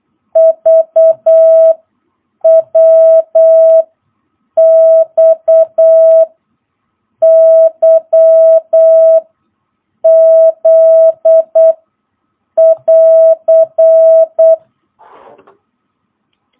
Morse code, heard as a keyed tone, about 8 wpm, 650 Hz. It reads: V W X Y Z <AR>
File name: jh8cbh / 2022cw